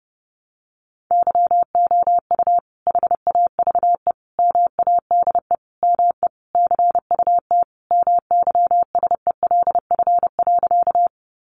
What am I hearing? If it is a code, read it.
Morse code, heard as a keyed tone, about 30 wpm, 700 Hz. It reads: YOU HAVE MADE ME CUT MYSELF.
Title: Sentences from Top 1300 Words 30wpm